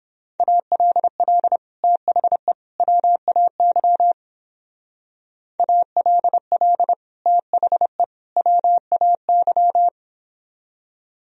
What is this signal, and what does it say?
Morse code, heard as a keyed tone, about 29 wpm, 700 Hz. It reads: ALL THE WAY ALL THE WAY